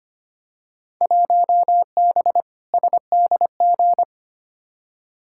Morse code, heard as a keyed tone, about 25 wpm, 700 Hz. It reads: 1B SDG
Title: Field Day Exchanges Encore - 25wpm